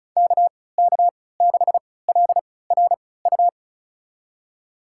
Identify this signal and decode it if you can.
Morse code, heard as a keyed tone, about 36 wpm, spaced as for 20 wpm, 700 Hz. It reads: KK6LRU
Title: USA Call Signs - 6 Characters - 3x Character Spacing - 35wpm